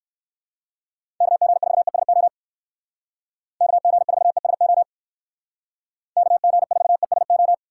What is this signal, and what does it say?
Morse code, heard as a keyed tone, about 59 wpm, 700 Hz. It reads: KG4ESO KG4ESO KG4ESO